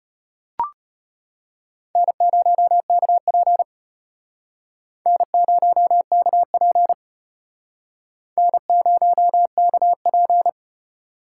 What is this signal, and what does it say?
Morse code, heard as a keyed tone, about 34 wpm, 700 Hz. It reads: N0KP N0KP N0KP